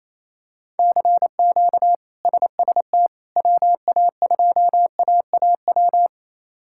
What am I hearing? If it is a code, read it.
Morse code, heard as a keyed tone, about 27 wpm, 700 Hz. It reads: CQ SST WA2AAW
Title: K1USN SST QSO Practice - 28wpm